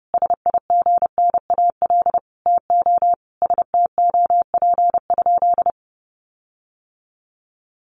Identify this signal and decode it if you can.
Morse code, heard as a keyed tone, about 28 wpm, 700 Hz. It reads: SIGNAL TO STOP?
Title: Sentences from Top 1400 Words 30wpm